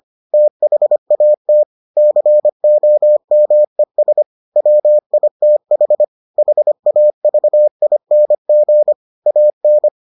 Morse code, heard as a keyed tone, about 25 wpm, 600 Hz. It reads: THAT COMES WITH HAVING AN